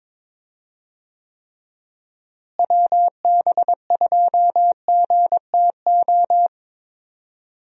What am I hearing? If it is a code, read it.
Morse code, heard as a keyed tone, about 22 wpm, 700 Hz. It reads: WB2GTO
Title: USA Call Signs - 6 Characters - Repeated 3x - 22wpm